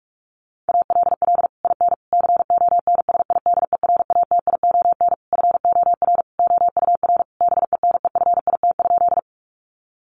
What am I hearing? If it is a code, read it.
Morse code, heard as a keyed tone, about 44 wpm, 700 Hz. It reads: ALL IN CONSIDERATION FOR OUR BENEFIT?
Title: Sentences from Top 1500 Words 45wpm